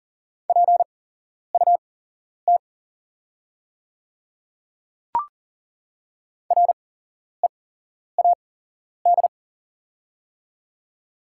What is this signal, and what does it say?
Morse code, heard as a keyed tone, about 40 wpm, spaced as for 11 wpm, 690 Hz. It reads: PUT READ
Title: 8x Character Spacing - Top 500 Words - 40wpm